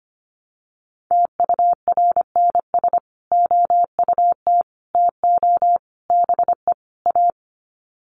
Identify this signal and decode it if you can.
Morse code, heard as a keyed tone, about 25 wpm, 700 Hz. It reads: TURNS OUT TO BE A
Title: Sets of 5 Words Encore 25wpm